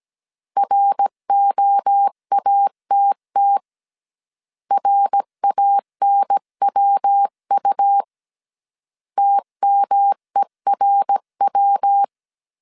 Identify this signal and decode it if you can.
Morse code, heard as a keyed tone, about 17 wpm, 800 Hz. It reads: ROATT RANWU TMERW